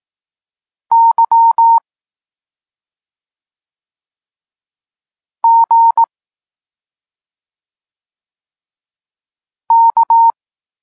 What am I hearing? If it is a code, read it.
Morse code, heard as a keyed tone, about 18 wpm, 920 Hz. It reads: Y G K
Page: Tekst 32 – Doskonalenie odbioru poznanych liter